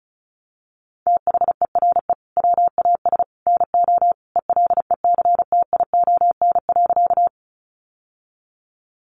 Morse code, heard as a keyed tone, about 34 wpm, 700 Hz. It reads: THERE WAS NO ELECTION.
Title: Sentences from Top 1500 Words 35wpm